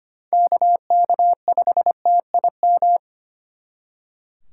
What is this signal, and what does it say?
Morse code, heard as a keyed tone, about 33 wpm, 700 Hz. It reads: KK5TIM